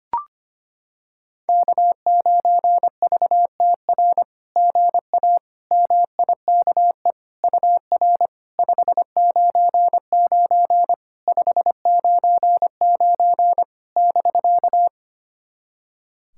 Morse code, heard as a keyed tone, about 25 wpm, 700 Hz. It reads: K9VTR GA MIKE UR 599 599 <BK>
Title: SOTA QSO Practice 25wpm